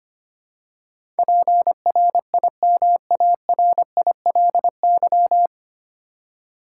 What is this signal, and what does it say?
Morse code, heard as a keyed tone, about 25 wpm, 700 Hz. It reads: PRIMARILY
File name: Nine Letter Words - 25wpm